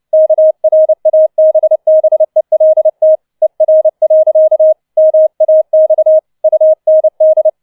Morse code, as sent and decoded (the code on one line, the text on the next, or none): -.- .-. .- -... -... .-.. - / . .-. .-.-.- / -- .- -..- / ..- -. -..
KRABBELT ER. MAX UND